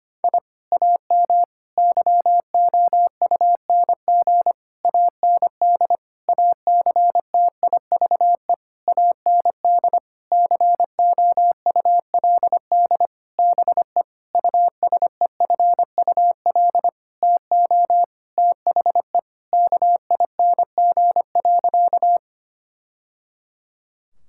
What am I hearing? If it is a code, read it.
Morse code, heard as a keyed tone, about 25 wpm, 700 Hz. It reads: I AM YOUNG AND ACTIVE AND COULD BE USEFUL TO THE KING.
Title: Sentences from Top 1000 Words - Review Part 6 - 25wpm